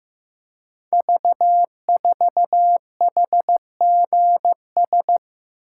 Morse code, heard as a keyed tone, about 15 wpm, 700 Hz. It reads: V4HGS
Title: Code Groups - 5 Letter:Numbers - 15wpm